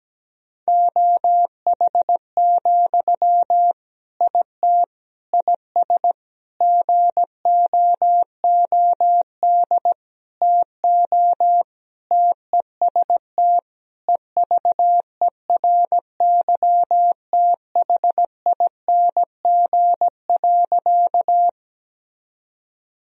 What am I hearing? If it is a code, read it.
Morse code, heard as a keyed tone, about 17 wpm, 700 Hz. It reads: OH, IT IS GOOD TO TEST EVERYTHING.
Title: Sentences from Top 1000 Words - Review Part 3 - 17wpm